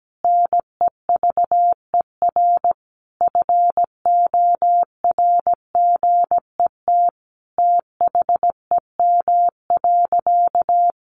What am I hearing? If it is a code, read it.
Morse code, heard as a keyed tone, about 17 wpm, 700 Hz. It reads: NEVER FORGET THEM.
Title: Sentences from Top 1200 Words 17wpm